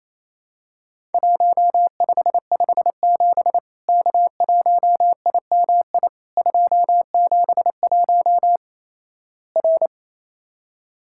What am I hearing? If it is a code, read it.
Morse code, heard as a keyed tone, about 28 wpm, 700 Hz. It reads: 1557 K1IMI 271 R
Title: Worked All Europe DX Contest - QTC Exchange - 28wpm